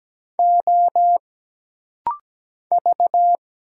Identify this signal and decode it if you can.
Morse code, heard as a keyed tone, about 17 wpm, 700 Hz. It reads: O V
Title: Single Letters - Rapid-Fire 17wpm